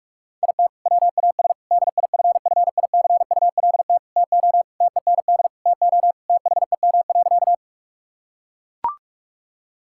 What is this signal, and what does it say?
Morse code, heard as a keyed tone, about 41 wpm, 700 Hz. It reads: IT WAS DIFFICULT TO TEND TO THEM.
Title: Sentences from Top 1000 Words - Review Part 5 - 45wpm